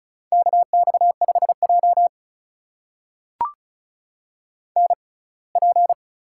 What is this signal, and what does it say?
Morse code, heard as a keyed tone, about 36 wpm, spaced as for 24 wpm, 700 Hz. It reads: KX5J NP
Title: USA Call Signs - 6x Character Spacing - 35wpm